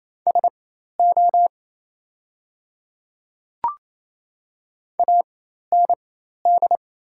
Morse code, heard as a keyed tone, about 27 wpm, spaced as for 13 wpm, 690 Hz. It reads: SO AND
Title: 4x Character Spacing - Top 100 Words - 28wpm